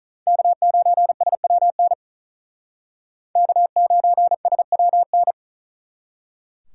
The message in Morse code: K9SWN K9SWN

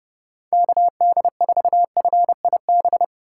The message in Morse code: KD4FIB

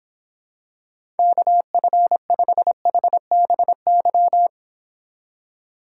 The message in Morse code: KF5HBY

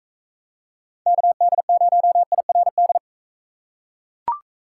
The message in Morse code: KD0IRD

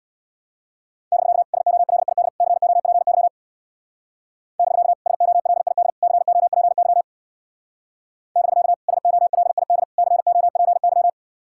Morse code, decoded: <BK> SOLID COPY <BK> SOLID COPY <BK> SOLID COPY

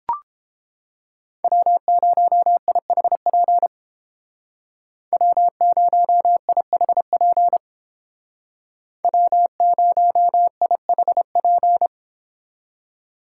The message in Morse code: W0IHP W0IHP W0IHP

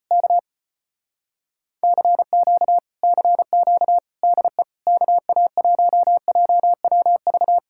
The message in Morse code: K CQ CQ DE KA1JWV

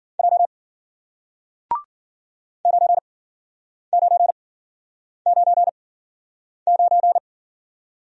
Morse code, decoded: J 9999

